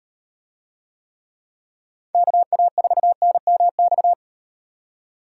Morse code, heard as a keyed tone, about 36 wpm, 700 Hz. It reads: KA4NMX